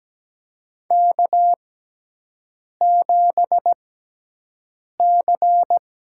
Morse code, heard as a keyed tone, about 17 wpm, spaced as for 6 wpm, 700 Hz. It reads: K7C
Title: USA Call Signs - 6x Character Spacing - 17wpm